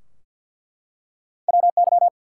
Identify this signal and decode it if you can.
Morse code, heard as a keyed tone, about 48 wpm, 700 Hz. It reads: WY
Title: US States and Canadian Provinces 50wpm